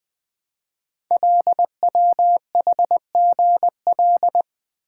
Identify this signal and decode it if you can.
Morse code, heard as a keyed tone, about 20 wpm, 700 Hz. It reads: LWHGL
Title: Code Groups - 5 Letter:Numbers - 20wpm